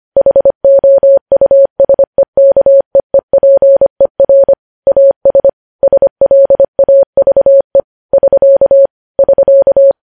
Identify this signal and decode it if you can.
Morse code, heard as a keyed tone, about 25 wpm, 550 Hz. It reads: HOUSEKEEPER AS SLAVE <SK> <SK>